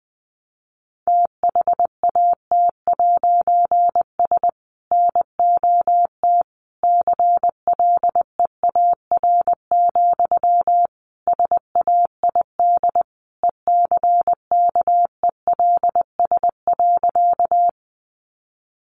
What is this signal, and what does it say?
Morse code, heard as a keyed tone, about 20 wpm, 700 Hz. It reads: THAT'S NOT CLEAR, SAID ECKELS.
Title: A Sound Of Thunder 20wpm